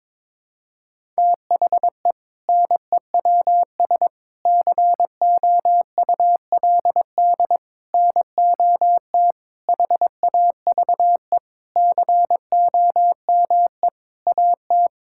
Morse code, heard as a keyed tone, about 21 wpm, 700 Hz. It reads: THE NEWS COULD NOT HAVE COME AT
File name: practice / 22wpm / Sentences with 8-Word Phrases - 22wpm